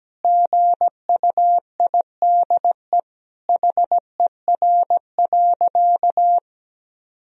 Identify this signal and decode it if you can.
Morse code, heard as a keyed tone, about 17 wpm, 700 Hz. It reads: GUIDE HER.
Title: Sentences from Top 1000 Words - Review Part 4 - 17wpm